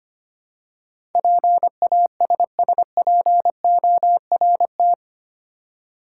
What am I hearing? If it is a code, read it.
Morse code, heard as a keyed tone, about 25 wpm, 700 Hz. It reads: PASSPORT
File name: 8 Letter Words - First 3 Letters Given - 25wpm